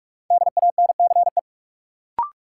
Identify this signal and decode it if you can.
Morse code, heard as a keyed tone, about 46 wpm, 700 Hz. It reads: DANCE
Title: Sentences from Top 700 Words - New Word Drill - 45wpm